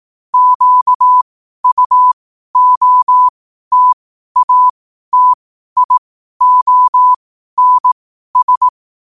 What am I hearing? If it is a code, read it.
Morse code, heard as a keyed tone, about 18 wpm, spaced as for 12 wpm, 1000 Hz. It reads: QUOTATIONS
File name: qotdfiles / 12wpm